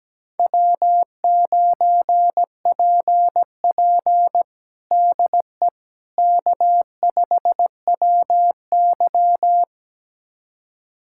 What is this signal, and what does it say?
Morse code, heard as a keyed tone, about 17 wpm, 700 Hz. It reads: W9PP DE K5WY
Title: USA Callsign DE Callsign - 4 Characters - 17wpm